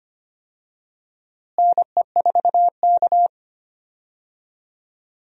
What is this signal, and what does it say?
Morse code, heard as a keyed tone, about 25 wpm, 700 Hz. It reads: NE4K
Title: USA Call Signs - 4 Characters - Repeated 3x - 25wpm